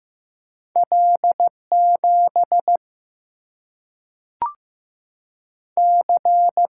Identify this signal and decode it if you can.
Morse code, heard as a keyed tone, about 15 wpm, 700 Hz. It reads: L 7 C
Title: Single-Letter Single-Number 15wpm